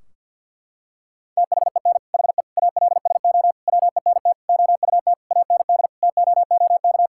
Morse code, heard as a keyed tone, about 52 wpm, 700 Hz. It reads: THEN HE ALSO WENT OUT AND TOOK
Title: Sentences from Top 1600 Words 50wpm